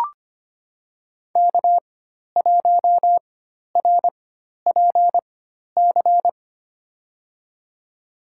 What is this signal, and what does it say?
Morse code, heard as a keyed tone, about 25 wpm, spaced as for 12 wpm, 700 Hz. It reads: K1RPC
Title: USA Call Signs - 5 Characters - 4x Character Spacing - 25wpm